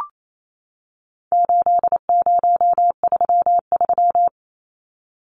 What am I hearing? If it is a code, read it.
Morse code, heard as a keyed tone, about 28 wpm, 700 Hz. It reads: 8033
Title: Numbers - 4-Digit 28wpm